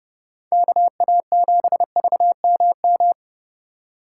Morse code, heard as a keyed tone, about 31 wpm, 700 Hz. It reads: KA7VMM
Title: USA Call Signs - 6 Characters - 4x Character Spacing - 30wpm